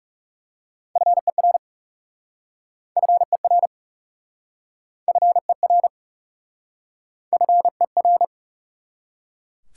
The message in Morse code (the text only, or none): FER FER FER FER